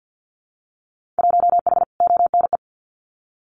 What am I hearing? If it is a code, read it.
Morse code, heard as a keyed tone, about 48 wpm, 700 Hz. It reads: 1H ONE
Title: Winter Field Day Exchanges 50wpm